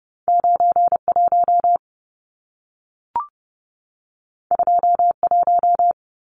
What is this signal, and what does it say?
Morse code, heard as a keyed tone, about 30 wpm, 700 Hz. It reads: 91 21
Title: Numbers - 2-Digit 30wpm